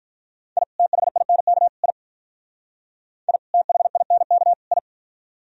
Morse code, heard as a keyed tone, about 45 wpm, 700 Hz. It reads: I THINK I I THINK I